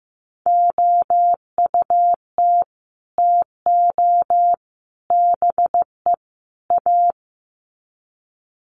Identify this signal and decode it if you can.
Morse code, heard as a keyed tone, about 15 wpm, 700 Hz. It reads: OUT TO BE A
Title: Sets of 4 Words 15wpm